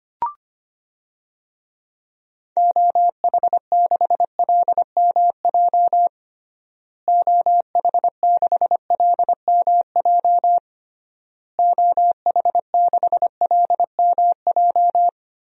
Morse code, heard as a keyed tone, about 25 wpm, 700 Hz. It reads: OH6LMJ OH6LMJ OH6LMJ
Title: DXCC-100 Call Signs - 6 Characters - Repeated 3x - 25wpm